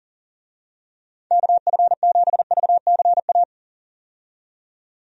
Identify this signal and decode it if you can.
Morse code, heard as a keyed tone, about 40 wpm, 700 Hz. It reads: KF7VCA
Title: USA Call Signs - 6 Characters - 40wpm